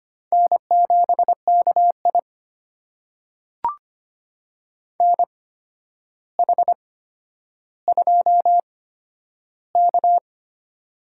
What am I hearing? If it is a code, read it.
Morse code, heard as a keyed tone, about 25 wpm, spaced as for 11 wpm, 700 Hz. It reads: N7KI NH2K
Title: USA Call Signs - 4 Characters - 8x Character Spacing - 25wpm